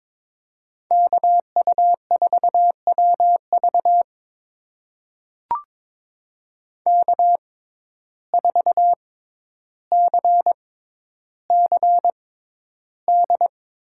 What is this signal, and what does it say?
Morse code, heard as a keyed tone, about 22 wpm, spaced as for 11 wpm, 700 Hz. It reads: KU4WV K4CCD
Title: USA Call Signs - 5 Characters - 6x Character Spacing - 22wpm